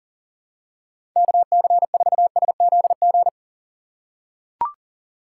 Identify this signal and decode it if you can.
Morse code, heard as a keyed tone, about 40 wpm, 700 Hz. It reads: KC4SZG
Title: USA Call Signs - 6 Characters - 3x Character Spacing - 40wpm